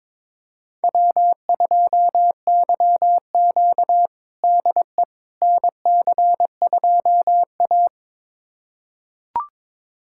Morse code, heard as a keyed tone, about 22 wpm, 700 Hz. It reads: W2YQ DE NC2A
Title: USA Callsign DE Callsign - 4 Characters - 22wpm